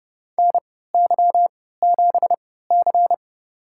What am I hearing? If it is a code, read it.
Morse code, heard as a keyed tone, about 30 wpm, spaced as for 17 wpm, 700 Hz. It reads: NY7C
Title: USA Call Signs - 3x Character Spacing - 30wpm